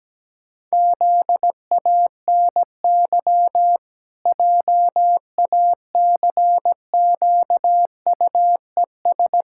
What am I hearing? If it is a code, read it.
Morse code, heard as a keyed tone, about 17 wpm, 700 Hz. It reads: ZANY JACQUES
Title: Pangram Challenge v2 - 17wpm